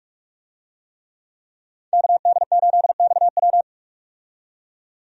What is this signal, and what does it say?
Morse code, heard as a keyed tone, about 46 wpm, 700 Hz. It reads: KD8XW
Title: USA Call Signs - 5 Characters - 45wpm